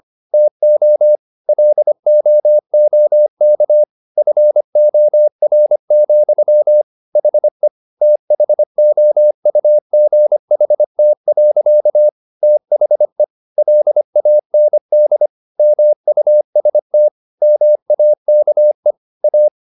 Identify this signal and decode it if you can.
Morse code, heard as a keyed tone, about 25 wpm, 600 Hz. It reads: TO LOOK FOR, HE THOUGHT. THE LAND MUST MAKE A
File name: the old man and the sea part 1 25wpm